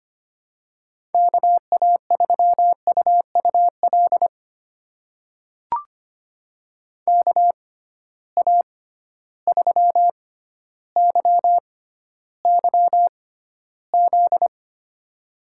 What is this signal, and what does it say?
Morse code, heard as a keyed tone, about 25 wpm, spaced as for 13 wpm, 700 Hz. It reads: KA3UUL KA3YYZ
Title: USA Call Signs - 6 Characters - 6x Character Spacing - 25wpm